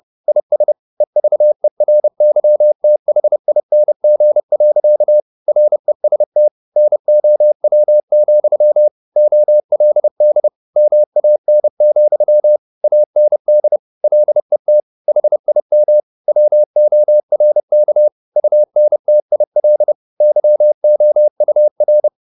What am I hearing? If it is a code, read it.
Morse code, heard as a keyed tone, about 30 wpm, 600 Hz. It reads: IS EVERYTHING. REST NOW, OLD MAN, AND LET HIM WORK UNTIL YOUR